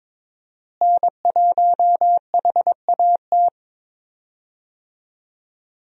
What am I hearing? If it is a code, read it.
Morse code, heard as a keyed tone, about 21 wpm, 700 Hz. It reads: N1HAT
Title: USA Call Signs - 5 Characters - Repeated 2x - 22wpm